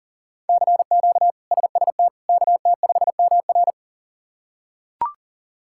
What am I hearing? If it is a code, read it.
Morse code, heard as a keyed tone, about 40 wpm, 690 Hz. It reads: CQ SST KT5MR
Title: K1USN SST QSO Practice - 40wpm